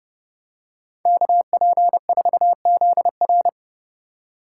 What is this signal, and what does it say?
Morse code, heard as a keyed tone, about 29 wpm, 700 Hz. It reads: KP4ZR
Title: USA Call Signs - 5 Characters - 3x Character Spacing - 30wpm